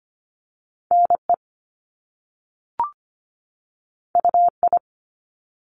Morse code, heard as a keyed tone, about 25 wpm, 700 Hz. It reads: NE UI
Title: Top Two Letter Combinations in Words 25wpm